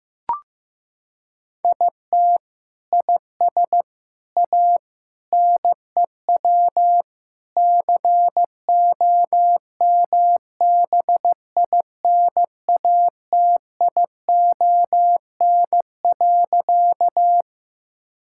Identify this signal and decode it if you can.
Morse code, heard as a keyed tone, about 15 wpm, 700 Hz. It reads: IT IS A NEW COMBINATION.
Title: Sentences from Top 1000 Words - Review Part 6 - 15wpm